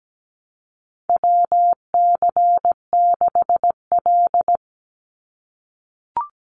Morse code, heard as a keyed tone, about 17 wpm, 700 Hz. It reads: WC6L
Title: USA Call Signs 17wpm